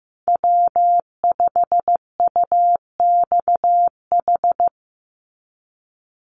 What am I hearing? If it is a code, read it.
Morse code, heard as a keyed tone, about 15 wpm, 700 Hz. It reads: W5UXH
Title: Call Signs Repeated 2x - 15wpm